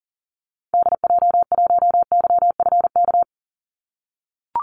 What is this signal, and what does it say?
Morse code, heard as a keyed tone, about 40 wpm, 700 Hz. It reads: DJ1YFK
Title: Call Signs - 40wpm